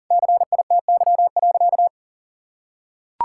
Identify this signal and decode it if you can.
Morse code, heard as a keyed tone, about 40 wpm, 700 Hz. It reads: CITY.
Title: Sentences from Top 1600 Words 40wpm